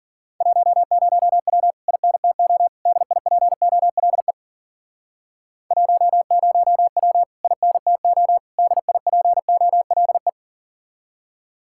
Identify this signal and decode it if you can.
Morse code, heard as a keyed tone, about 50 wpm, 700 Hz. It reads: 10W INTO DIPOLE 10W INTO DIPOLE
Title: Speed-Racing Common QSO Phrases - 40wpm